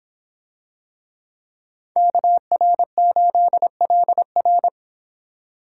Morse code, heard as a keyed tone, about 26 wpm, 700 Hz. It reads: KR8LR